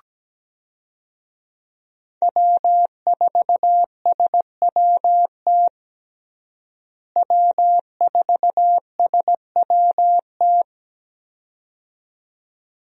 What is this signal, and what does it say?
Morse code, heard as a keyed tone, about 17 wpm, 700 Hz. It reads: W4SWT W4SWT
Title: USA Call Signs - 5 Characters - Repeated 2x - 17wpm